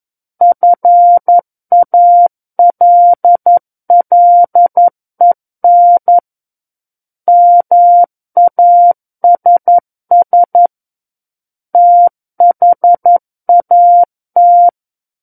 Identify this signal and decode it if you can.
Morse code, heard as a keyed tone, about 11 wpm, 700 Hz. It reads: FALLEN MASS THAT
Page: SKCC - War of the Worlds - 02-Chapter 2 - 11 WPM